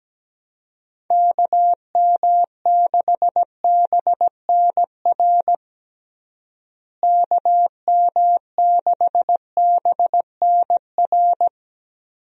Morse code, heard as a keyed tone, about 17 wpm, 700 Hz. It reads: KM6BNR KM6BNR
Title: USA Call Signs - 6 Characters - Repeated 3x - 17wpm